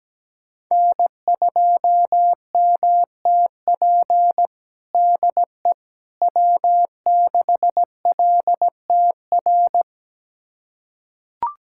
Morse code, heard as a keyed tone, about 17 wpm, 700 Hz. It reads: N2MTP DE W6LTR
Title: USA Callsign DE Callsign - 5 Characters - 17wpm